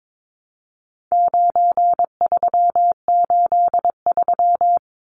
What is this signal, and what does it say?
Morse code, heard as a keyed tone, about 22 wpm, 700 Hz. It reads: 9383
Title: Numbers - 4-Digit 22wpm